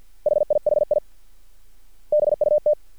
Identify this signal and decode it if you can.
Morse code, heard as a keyed tone, about 50 wpm, 600 Hz. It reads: HIHI BUT
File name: archived / CRSnet